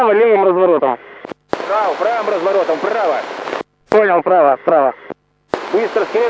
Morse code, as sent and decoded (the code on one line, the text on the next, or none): none